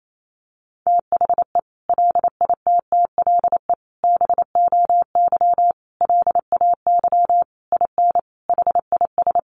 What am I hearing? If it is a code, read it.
Morse code, heard as a keyed tone, about 28 wpm, 700 Hz. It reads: THE LITTLE BOY LAY IN HIS
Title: Sentences from Top 500 Words 28wpm